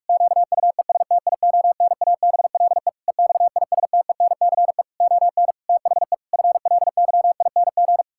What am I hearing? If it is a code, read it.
Morse code, heard as a keyed tone, about 41 wpm, 700 Hz. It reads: QUESTIONABLE EXISTENCE ON THE FLYING
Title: Pangram Challenge v2 - 45wpm